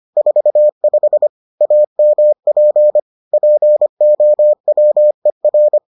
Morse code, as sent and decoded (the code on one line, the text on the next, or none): ....- ..... / .- -- .--. / .--. --- .-- . .-.
45 AMP POWER